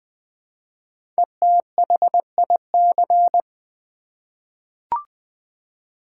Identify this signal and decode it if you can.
Morse code, heard as a keyed tone, about 20 wpm, 690 Hz. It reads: ETHIC
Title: 5 Letter Words - First 3 Letters Given - 20wpm